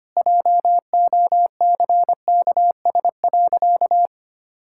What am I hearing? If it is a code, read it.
Morse code, heard as a keyed tone, about 25 wpm, 700 Hz. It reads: JOCKS.
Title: Pangram Challenge v2 - 25wpm